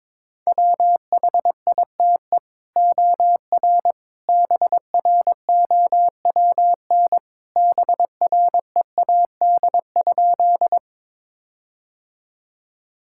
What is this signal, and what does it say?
Morse code, heard as a keyed tone, about 22 wpm, 700 Hz. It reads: WHITE OR BROWN BREAD?
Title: Sentences from Top 1600 Words 22wpm